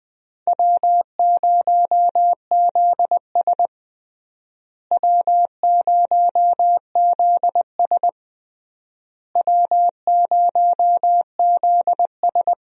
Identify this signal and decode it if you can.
Morse code, heard as a keyed tone, about 20 wpm, 700 Hz. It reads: W0ZS W0ZS W0ZS
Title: USA Call Signs - 4 Characters - Repeated 3x - 20wpm